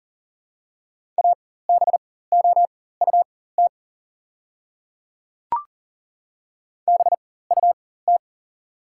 Morse code, heard as a keyed tone, about 40 wpm, spaced as for 19 wpm, 700 Hz. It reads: ABOUT BUT
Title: 4x Character Spacing - Top 200 Words - 40wpm